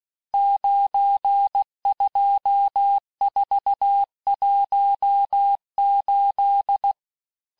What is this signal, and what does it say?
Morse code, heard as a keyed tone, about 16 wpm, 800 Hz. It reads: 92418
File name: cours-du-lundi files / cours-1592--teste-chiffre-vitesse-15mpm